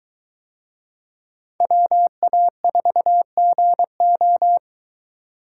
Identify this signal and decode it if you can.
Morse code, heard as a keyed tone, about 21 wpm, 700 Hz. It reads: WA4GO